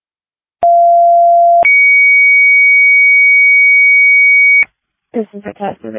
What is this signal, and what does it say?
Two-tone page, heard as a one-second tone, then a three-second tone, about 690 then 2250 Hz, with speech. It is heard from 0.6 to 4.6 s.
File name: dispatch / Denver Fire 2026 07 16 19 00 19